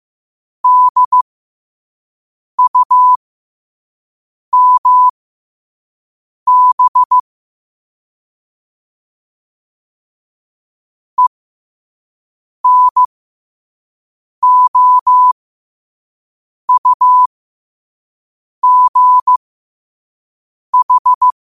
Morse code, heard as a keyed tone, about 15 wpm, spaced as for 5 wpm, 1000 Hz. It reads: DUMB ENOUGH
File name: qotdfiles / 5wpm